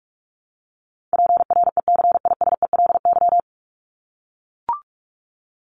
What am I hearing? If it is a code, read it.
Morse code, heard as a keyed tone, about 44 wpm, 700 Hz. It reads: PRECISELY
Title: LY Suffix - Top 100 Words with Base Words 45wpm